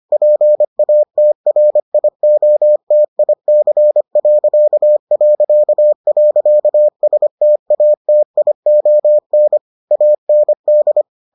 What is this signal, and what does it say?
Morse code, heard as a keyed tone, about 25 wpm, 600 Hz. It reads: PATRIOTIC...STATION AND